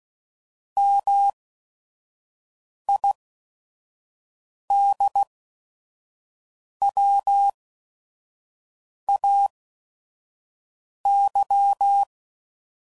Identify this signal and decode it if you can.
Morse code, heard as a keyed tone, about 16 wpm, spaced as for 5 wpm, 800 Hz. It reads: MIDWAY
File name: CW Code Groups @5WPM / fromSM's / Cities